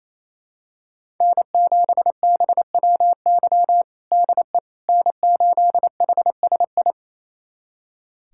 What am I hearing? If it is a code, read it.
Morse code, heard as a keyed tone, about 27 wpm, 700 Hz. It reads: N7BWY DE N8HSI
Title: USA Callsign DE Callsign - 5 Characters - 28wpm